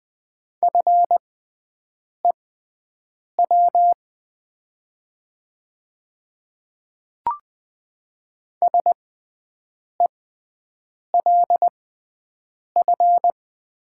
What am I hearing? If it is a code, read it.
Morse code, heard as a keyed tone, about 20 wpm, spaced as for 7 wpm, 700 Hz. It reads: FEW SELF
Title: 6x Character Spacing - Top 500 Words - 20wpm